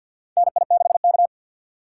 Morse code, heard as a keyed tone, about 52 wpm, 700 Hz. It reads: NI6K
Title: USA Call Signs - 4 Characters - 2x Character Spacing - 50wpm